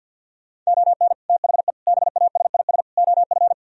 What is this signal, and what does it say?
Morse code, heard as a keyed tone, about 63 wpm, 700 Hz. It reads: ON THE BASIS OF